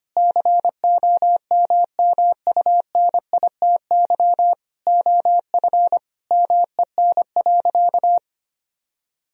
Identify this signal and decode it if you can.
Morse code, heard as a keyed tone, about 25 wpm, 700 Hz. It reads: COMMUNITY OF MEN.
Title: Sentences from Top 1000 Words - Review Part 4 - 25wpm